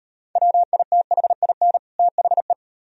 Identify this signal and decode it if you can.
Morse code, heard as a keyed tone, about 38 wpm, 700 Hz. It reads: WITHIN THE